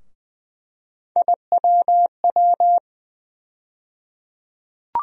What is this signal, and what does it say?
Morse code, heard as a keyed tone, about 20 wpm, 700 Hz. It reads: IWW
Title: USA Call Sign Postfixes - 3 Letters 20wpm